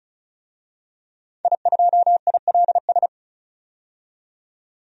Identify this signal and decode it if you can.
Morse code, heard as a keyed tone, about 36 wpm, 700 Hz. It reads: I2ILS
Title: DXCC-100 Call Signs - 5 Characters - 35wpm